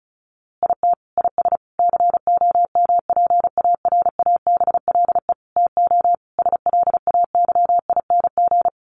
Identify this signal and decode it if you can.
Morse code, heard as a keyed tone, about 34 wpm, 700 Hz. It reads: IT IS COMPARABLE TO SLAYING
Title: A Sound Of Thunder 35wpm